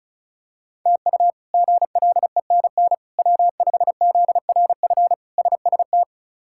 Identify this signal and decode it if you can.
Morse code, heard as a keyed tone, about 36 wpm, 700 Hz. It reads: TU GLENN W5ZRF SST
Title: K1USN SST QSO Practice - 35wpm